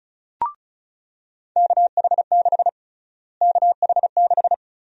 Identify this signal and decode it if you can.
Morse code, heard as a keyed tone, about 36 wpm, 700 Hz. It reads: KH6 KH6
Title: USA Call Sign Prefixes - Repeated 2x - 35wpm